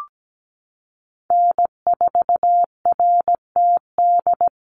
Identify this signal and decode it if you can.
Morse code, heard as a keyed tone, about 17 wpm, 700 Hz. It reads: N4RTD
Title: USA Call Signs Repeated 3x 17wpm